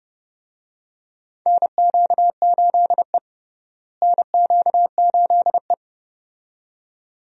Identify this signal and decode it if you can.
Morse code, heard as a keyed tone, about 31 wpm, 700 Hz. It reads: NQ8E NQ8E
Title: USA Call Signs - 4 Characters - Repeated 2x - 30wpm